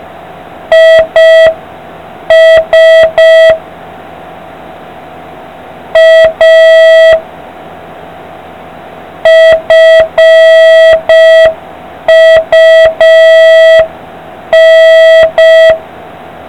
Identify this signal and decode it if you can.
Morse code, heard as a keyed tone, about 5 wpm, 1950 Hz. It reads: IS A FUN